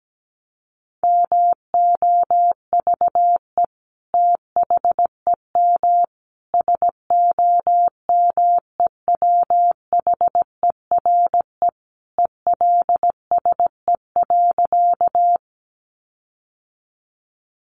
Morse code, heard as a keyed tone, about 17 wpm, 700 Hz. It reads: MOVE THEM SOMEWHERE ELSE.
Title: Sentences from Top 1400 Words 17wpm